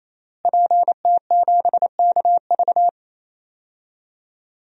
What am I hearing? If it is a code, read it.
Morse code, heard as a keyed tone, about 28 wpm, 700 Hz. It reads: PT7KV
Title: DXCC-100 Call Signs - 5 Characters - Repeated 3x - 28wpm